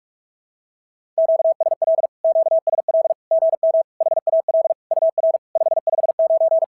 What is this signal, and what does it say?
Morse code, heard as a keyed tone, about 46 wpm, 650 Hz. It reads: QSL QSL GM HAL UR 559
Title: SOTA QSO Practice 45wpm